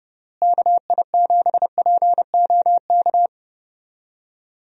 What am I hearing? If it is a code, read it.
Morse code, heard as a keyed tone, about 32 wpm, 700 Hz. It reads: KI7POK
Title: USA Call Signs - 6 Characters - Repeated 3x - 30wpm